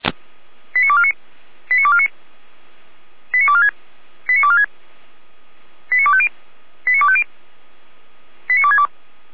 Five-tone selective call (ZVEI), heard as quick runs of five tones, tones of about 70 ms. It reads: 89249 89249 89246 89246 89250 89250 89262